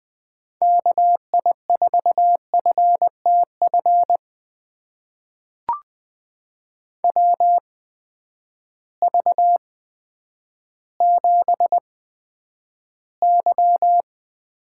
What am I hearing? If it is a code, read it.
Morse code, heard as a keyed tone, about 20 wpm, spaced as for 10 wpm, 700 Hz. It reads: KI4FTF WV7Y
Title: USA Call Signs - 8x Character Spacing - 20wpm